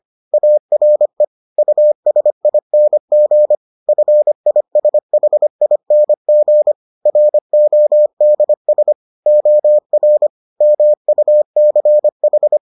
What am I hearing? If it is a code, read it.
Morse code, heard as a keyed tone, about 25 wpm, 600 Hz. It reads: ARE USING FISHING RODS OR MUCH